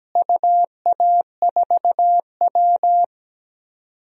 Morse code, heard as a keyed tone, about 17 wpm, 700 Hz. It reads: UA4W
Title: DXCC-100 Call Signs - 4 Characters - Repeated 3x - 17wpm